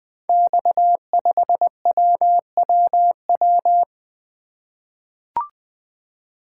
Morse code, heard as a keyed tone, about 20 wpm, 700 Hz. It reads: X5WWW
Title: Code Groups - 5 Letter:Numbers - 20wpm